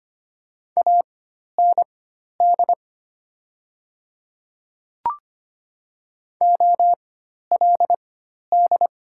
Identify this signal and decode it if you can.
Morse code, heard as a keyed tone, about 25 wpm, spaced as for 12 wpm, 700 Hz. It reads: AND OLD
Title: 4x Character Spacing - Top 200 Words - 25wpm